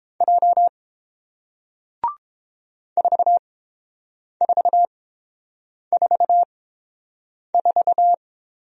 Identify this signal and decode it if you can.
Morse code, heard as a keyed tone, about 30 wpm, spaced as for 7 wpm, 700 Hz. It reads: J 4444